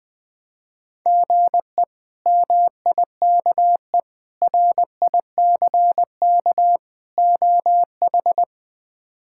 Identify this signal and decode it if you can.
Morse code, heard as a keyed tone, about 20 wpm, 700 Hz. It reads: GE MIKE RICK OH
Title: K1USN SST QSO Practice - 20wpm